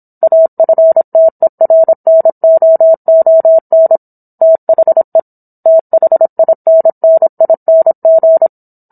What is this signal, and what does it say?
Morse code, heard as a keyed tone, about 27 wpm, 650 Hz. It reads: AFTERNOON THE THINNING